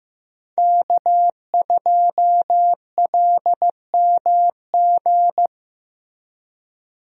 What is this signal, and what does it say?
Morse code, heard as a keyed tone, about 15 wpm, 700 Hz. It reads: K2LMG
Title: USA Call Signs - 5 Characters - Repeated 3x - 15wpm